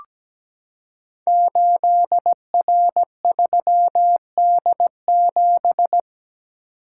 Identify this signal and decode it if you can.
Morse code, heard as a keyed tone, about 17 wpm, 700 Hz. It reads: 8R3D7
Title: Code Groups - 5 Letter:Numbers - 17wpm